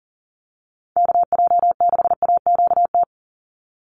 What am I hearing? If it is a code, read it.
Morse code, heard as a keyed tone, about 40 wpm, 700 Hz. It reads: KJ6AQT
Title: USA Call Signs 40wpm